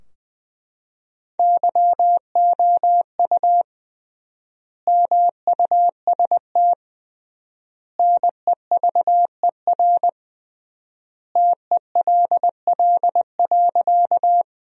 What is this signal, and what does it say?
Morse code, heard as a keyed tone, about 20 wpm, 700 Hz. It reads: YOU MUST NEVER TELL.
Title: 3x Word-Spacing - Sentences from Top 300 Words - 20wpm